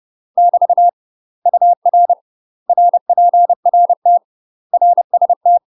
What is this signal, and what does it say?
Morse code, heard as a keyed tone, about 30 wpm, 700 Hz. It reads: <BT> UR RPRT RST
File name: UK MASTER QSOs TRAINING / 30WPM / UK MASTER QSOs 0000